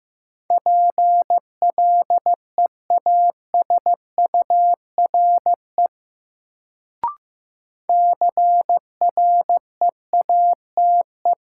Morse code, heard as a keyed tone, about 15 wpm, 700 Hz. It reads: PLEASURE CREATE
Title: Sentences from Top 1600 Words - New Word Drill Alt - 15wpm